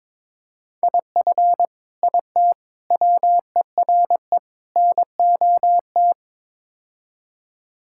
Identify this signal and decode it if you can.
Morse code, heard as a keyed tone, about 21 wpm, 700 Hz. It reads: IF IT WERE NOT
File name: practice / 22wpm / Build A Phrase - 5-Word Phrases - 22wpm